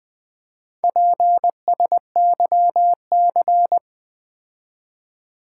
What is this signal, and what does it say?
Morse code, heard as a keyed tone, about 20 wpm, 700 Hz. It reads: PSYC
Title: Build A Word - 4 Letters At A Time - 10-Letter Words - 20wpm